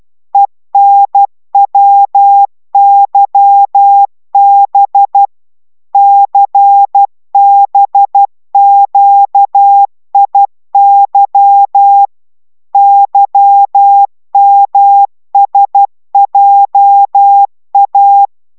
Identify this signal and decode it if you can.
Morse code, heard as a keyed tone, about 12 wpm, 800 Hz. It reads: ENWYB CBQIY YMSJA